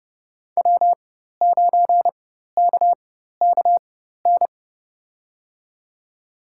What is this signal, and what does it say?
Morse code, heard as a keyed tone, about 30 wpm, spaced as for 14 wpm, 700 Hz. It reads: W9KKN
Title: USA Call Signs - 4x Character Spacing - 30wpm